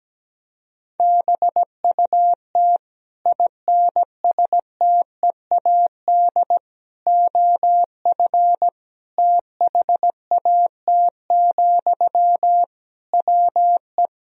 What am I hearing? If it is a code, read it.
Morse code, heard as a keyed tone, about 17 wpm, 700 Hz. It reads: BUT INSTEAD OF THAT, WE
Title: Sentences from Top 1000 Words - Review Part 4 - 17wpm